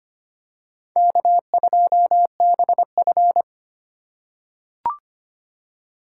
Morse code, heard as a keyed tone, about 25 wpm, 700 Hz. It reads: K2BF
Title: USA Call Signs - 4 Characters - 3x Character Spacing - 25wpm